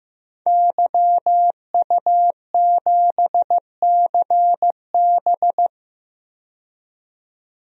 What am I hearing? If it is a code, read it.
Morse code, heard as a keyed tone, about 15 wpm, 700 Hz. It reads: YU7CB
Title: DXCC-100 Call Signs - 5 Characters - 15wpm